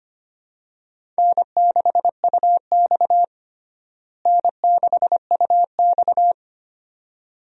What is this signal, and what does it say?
Morse code, heard as a keyed tone, about 25 wpm, 700 Hz. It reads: N6UX N6UX
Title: USA Call Signs - 4 Characters - Repeated 2x - 25wpm